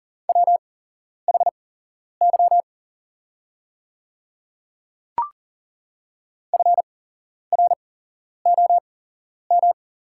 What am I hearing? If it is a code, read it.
Morse code, heard as a keyed tone, about 40 wpm, spaced as for 11 wpm, 690 Hz. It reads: WHY FROM
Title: 8x Character Spacing - Top 300 Words - 40wpm